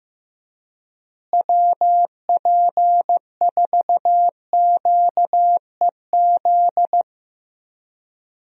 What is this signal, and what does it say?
Morse code, heard as a keyed tone, about 15 wpm, 700 Hz. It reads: WP4QEZ
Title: USA Call Signs - 6 Characters - Repeated 2x - 15wpm